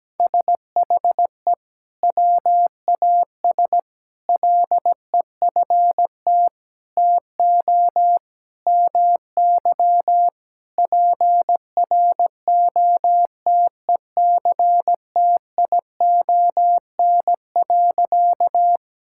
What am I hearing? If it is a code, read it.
Morse code, heard as a keyed tone, about 17 wpm, 700 Hz. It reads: SHE WAS LEFT TO MY PROTECTION.
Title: Sentences from Top 1600 Words 17wpm